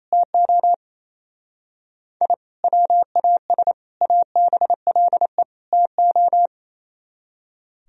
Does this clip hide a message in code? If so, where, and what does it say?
Yes: Morse code from 0.1 to 6.5 s, heard as a keyed tone, about 37 wpm, 700 Hz. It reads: TO I WAS ABLE TO